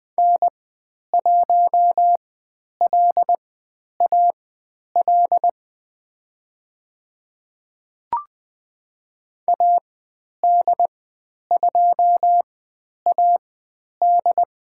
Morse code, heard as a keyed tone, about 21 wpm, spaced as for 10 wpm, 700 Hz. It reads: N1LAL AD2AD
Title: 109 Lesson 18g - Letter m - Call Signs 20wpm